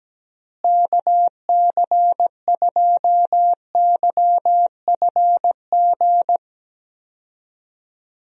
Morse code, heard as a keyed tone, about 17 wpm, 700 Hz. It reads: KC2YFG
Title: USA Call Signs - 6 Characters - Repeated 2x - 17wpm